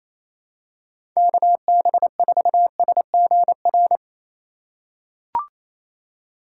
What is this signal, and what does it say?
Morse code, heard as a keyed tone, about 28 wpm, 700 Hz. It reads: KB4SGR
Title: USA Call Signs - 4x Character Spacing - 28wpm